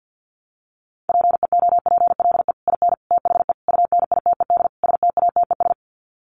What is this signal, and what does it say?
Morse code, heard as a keyed tone, about 50 wpm, 700 Hz. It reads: PEOPLE IN THE UNITED STATES
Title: Sets of 5 Words Encore 50wpm